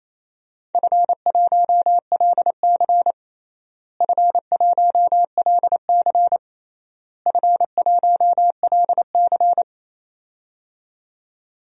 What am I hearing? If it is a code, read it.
Morse code, heard as a keyed tone, about 27 wpm, 700 Hz. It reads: F1LC F1LC F1LC
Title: DXCC-100 Call Signs - 4 Characters - Repeated 3x - 28wpm